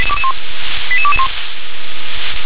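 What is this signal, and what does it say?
Five-tone selective call (ZVEI), heard as quick runs of five tones, tones of about 70 ms. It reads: 88221
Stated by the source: ZVEI five-tone selective call received on an SDR.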